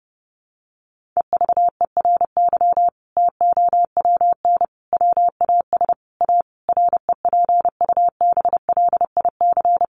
Morse code, heard as a keyed tone, about 30 wpm, 700 Hz. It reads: EVERY TOWN WAS A REPUBLIC
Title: Sentences from Top 1500 Words 30wpm